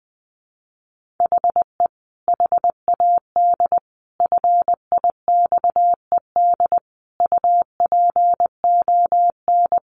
Morse code, heard as a keyed tone, about 20 wpm, 700 Hz. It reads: HE HAD FIXED UPON
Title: Sentences from Top 1500 Words 20wpm